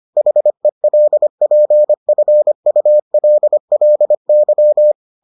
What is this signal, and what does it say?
Morse code, heard as a keyed tone, about 25 wpm, 600 Hz. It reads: HELPFULLY